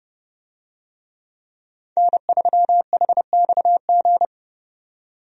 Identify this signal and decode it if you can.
Morse code, heard as a keyed tone, about 30 wpm, 700 Hz. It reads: N3HXG
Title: USA Call Signs - 5 Characters - 30wpm